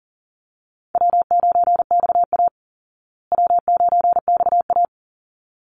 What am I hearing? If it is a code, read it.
Morse code, heard as a keyed tone, about 40 wpm, 700 Hz. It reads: W9XA W9XA
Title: Call Signs Repeated 3x - 40wpm